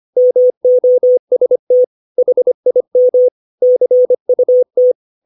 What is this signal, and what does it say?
Morse code, heard as a keyed tone, about 25 wpm, 500 Hz. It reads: MOST HIM CUT
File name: 632 words / 632 Words 25WPM